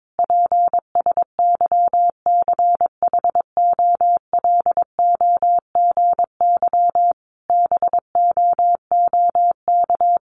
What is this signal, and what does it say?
Morse code, heard as a keyed tone, about 22 wpm, 700 Hz. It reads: PSYCHOLOGY BOOK
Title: Sentences from Top 1400 Words 22wpm